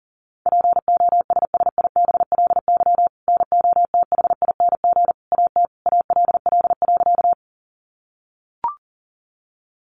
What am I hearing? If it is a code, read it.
Morse code, heard as a keyed tone, about 40 wpm, 700 Hz. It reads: POSSIBLY NOTHING AT ALL.
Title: Sentences from Top 1300 Words 40wpm